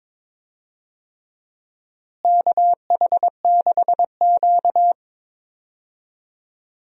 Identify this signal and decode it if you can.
Morse code, heard as a keyed tone, about 22 wpm, 700 Hz. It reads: KH6Q
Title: DXCC-100 Call Signs - 4 Characters - Repeated 3x - 22wpm